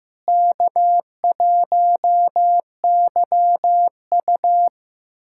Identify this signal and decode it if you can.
Morse code, heard as a keyed tone, about 15 wpm, 700 Hz. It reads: K1YU
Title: USA Callsign DE Callsign - 4 Characters - 15wpm